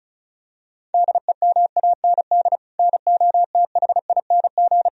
Morse code, heard as a keyed tone, about 34 wpm, 700 Hz. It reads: DEMAND NOTHING
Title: Sentences from Top 1000 Words - Review Part 4 - 35wpm